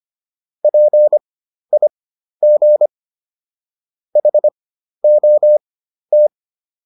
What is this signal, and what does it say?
Morse code, heard as a keyed tone, about 25 wpm, spaced as for 12 wpm, 600 Hz. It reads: PIG HOT